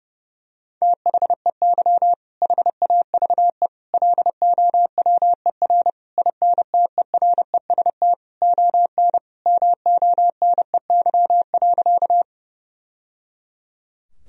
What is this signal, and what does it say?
Morse code, heard as a keyed tone, about 30 wpm, 700 Hz. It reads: THEY HAVE LOWER INTEREST ON MONEY.
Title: Sentences from Top 1000 Words - Review Part 4 - 30wpm